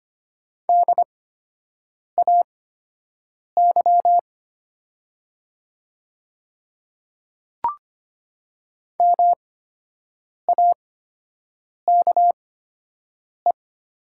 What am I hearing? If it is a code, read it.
Morse code, heard as a keyed tone, about 25 wpm, spaced as for 7 wpm, 700 Hz. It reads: DAY MAKE
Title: 8x Character Spacing - Top 100 Words - 25wpm